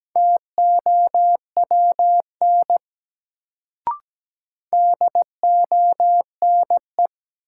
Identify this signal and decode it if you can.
Morse code, heard as a keyed tone, about 17 wpm, 700 Hz. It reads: TOWN DONE
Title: Sentences from Top 500 Words - New Word Drill - 17wpm